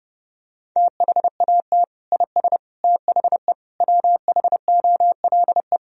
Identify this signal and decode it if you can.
Morse code, heard as a keyed tone, about 30 wpm, 700 Hz. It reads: THAT IS THE WHOLE
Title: Sentences from Top 1000 Words - Review Part 3 - 30wpm